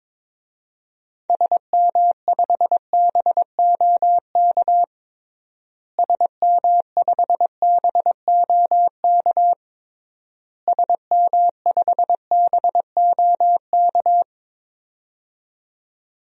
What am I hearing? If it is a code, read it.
Morse code, heard as a keyed tone, about 21 wpm, 700 Hz. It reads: SM5BOK SM5BOK SM5BOK
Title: DXCC-100 Call Signs - 6 Characters - Repeated 3x - 22wpm